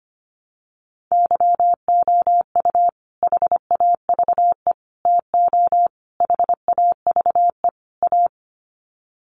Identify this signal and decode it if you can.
Morse code, heard as a keyed tone, about 25 wpm, 700 Hz. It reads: YOU HAVE TO HAVE A
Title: Sets of 5 Words Encore 25wpm